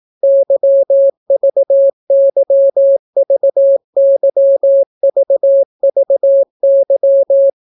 Morse code, heard as a keyed tone, about 18 wpm, 550 Hz. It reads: YVYVYVVY